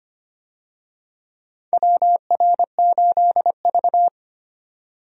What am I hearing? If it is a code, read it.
Morse code, heard as a keyed tone, about 25 wpm, 700 Hz. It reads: WR8V
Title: USA Call Signs - 4 Characters - 25wpm